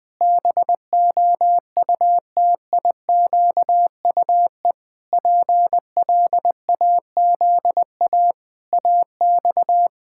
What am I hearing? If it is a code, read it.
Morse code, heard as a keyed tone, about 20 wpm, 700 Hz. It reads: BOUTIQUE PLAZA AX
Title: Pangram Challenge v2 - 20wpm